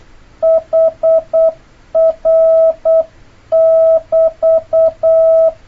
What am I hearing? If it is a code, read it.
Morse code, heard as a keyed tone, about 8 wpm, 650 Hz. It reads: HR <BT>
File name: jh8cbh / 2026cw